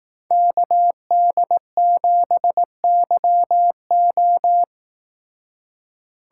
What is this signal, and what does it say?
Morse code, heard as a keyed tone, about 18 wpm, 700 Hz. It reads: KD7YO
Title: Speed Racing - USA 2x2 Call Signs 15wpm